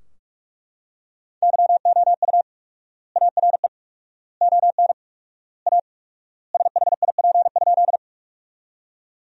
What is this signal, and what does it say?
Morse code, heard as a keyed tone, about 43 wpm, 700 Hz. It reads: YOU ARE ON A SHIP?
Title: 4x Word-Spacing - Sentences from Top 400 Words - 45wpm